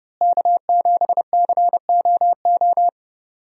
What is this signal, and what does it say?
Morse code, heard as a keyed tone, about 32 wpm, 700 Hz. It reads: K7COO
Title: USA Call Signs - 5 Characters - Repeated 3x - 30wpm